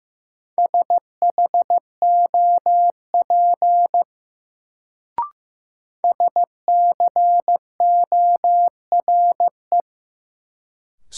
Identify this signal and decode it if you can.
Morse code, heard as a keyed tone, about 15 wpm, 700 Hz. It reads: SHOP SCORE
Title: Sentences from Top 1600 Words - New Word Drill - 15wpm